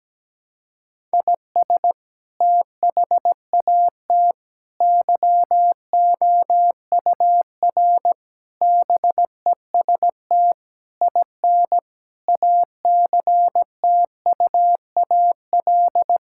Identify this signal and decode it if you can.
Morse code, heard as a keyed tone, about 17 wpm, 700 Hz. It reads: IS THAT YOUR BEST IN ACTUAL
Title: Sentences from Top 1000 Words - Review Part 5 - 17wpm